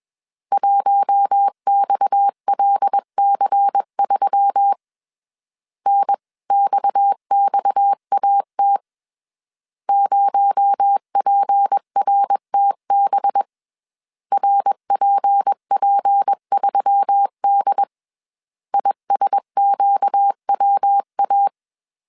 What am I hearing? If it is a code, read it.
Morse code, heard as a keyed tone, about 21 wpm, 800 Hz. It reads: 1XLC3 NXXAT 0PRTB RPP3D ISQWA
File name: CW / DET / 1 6 1500C S C 20wpm 2116